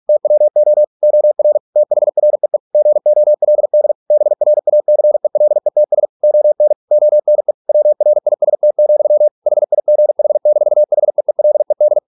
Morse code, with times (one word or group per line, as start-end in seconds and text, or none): 0.1-0.8 TWO
1.0-1.6 OR
1.8-2.6 THREE
2.7-3.9 GOLD
4.1-6.1 BRACELETS
6.2-6.7 ON
6.9-7.5 ONE
7.7-9.3 WRIST,
9.5-12.0 HIGH-HEELED